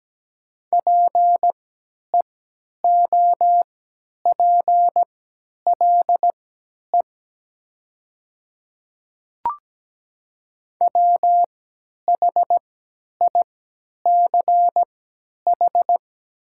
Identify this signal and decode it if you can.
Morse code, heard as a keyed tone, about 17 wpm, spaced as for 10 wpm, 700 Hz. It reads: PEOPLE WHICH
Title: 3x Character Spacing - Top 100 Words - 17wpm